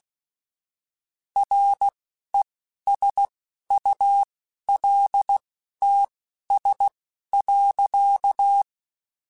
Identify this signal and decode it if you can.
Morse code, heard as a keyed tone, about 16 wpm, spaced as for 11 wpm, 800 Hz. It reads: RESULTS.